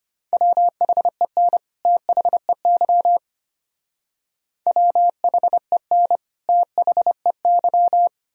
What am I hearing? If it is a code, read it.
Morse code, heard as a keyed tone, about 29 wpm, 700 Hz. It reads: WHEN THEY WHEN THEY
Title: Speed-Racing - Sets of 2 Words 25wpm